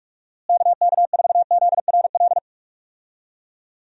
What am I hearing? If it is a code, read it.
Morse code, heard as a keyed tone, about 46 wpm, 700 Hz. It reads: KK4ZRL
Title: USA Call Signs - 6 Characters - Repeated 2x - 45wpm